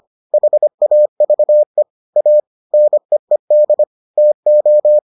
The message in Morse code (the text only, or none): HAVE A NEED TO